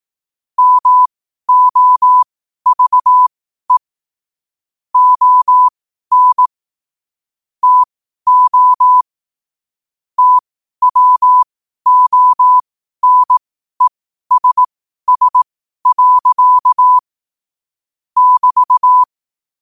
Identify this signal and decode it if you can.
Morse code, heard as a keyed tone, about 18 wpm, spaced as for 12 wpm, 1000 Hz. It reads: MOVE ON TO TWONESS. <BT>
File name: qotdfiles / 12wpm